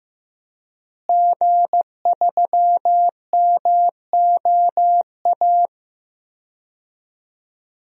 Morse code, heard as a keyed tone, about 15 wpm, 700 Hz. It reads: G3MOA
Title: DXCC-100 Call Signs - 5 Characters - Repeated 3x - 15wpm